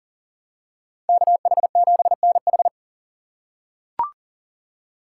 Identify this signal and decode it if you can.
Morse code, heard as a keyed tone, about 40 wpm, 700 Hz. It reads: KH7NH
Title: USA Call Signs - 5 Characters - 3x Character Spacing - 40wpm